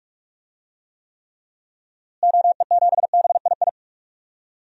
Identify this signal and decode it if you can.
Morse code, heard as a keyed tone, about 44 wpm, 700 Hz. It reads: OE7BII
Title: DXCC-100 Call Signs - 6 Characters - Repeated 3x - 45wpm